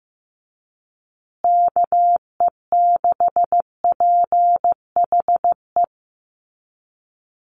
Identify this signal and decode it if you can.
Morse code, heard as a keyed tone, about 15 wpm, 700 Hz. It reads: KE6PHE
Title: USA Call Signs Repeated 3x 15wpm